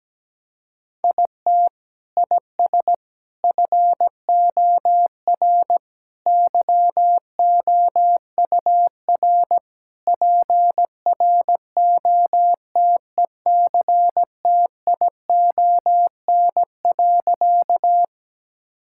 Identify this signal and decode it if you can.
Morse code, heard as a keyed tone, about 17 wpm, 700 Hz. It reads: IT IS FOR YOUR PROTECTION.
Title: Sentences from Top 1600 Words 17wpm